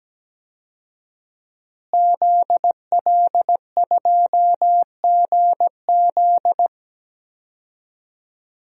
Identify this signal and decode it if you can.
Morse code, heard as a keyed tone, about 17 wpm, 700 Hz. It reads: ZL2GZ
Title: DXCC-100 Call Signs - 5 Characters - Repeated 3x - 17wpm